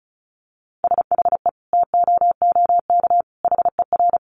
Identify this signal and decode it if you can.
Morse code, heard as a keyed tone, about 34 wpm, 700 Hz. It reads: SHE TOOK HER
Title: Sentences from Top 900 Words 35wpm